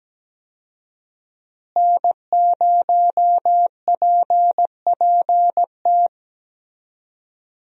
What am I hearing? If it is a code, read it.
Morse code, heard as a keyed tone, about 17 wpm, 700 Hz. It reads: N0PPT
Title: USA Call Signs - 5 Characters - Repeated 3x - 17wpm